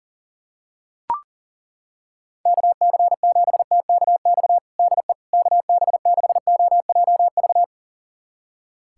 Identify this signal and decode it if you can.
Morse code, heard as a keyed tone, about 40 wpm, 700 Hz. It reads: KC7TKX DE KB6OJV
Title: USA Callsign DE Callsign - 6 Characters - 40wpm